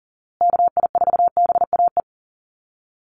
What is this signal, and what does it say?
Morse code, heard as a keyed tone, about 40 wpm, 700 Hz. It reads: KI4BAE
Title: USA Call Signs Repeated 2x 40wpm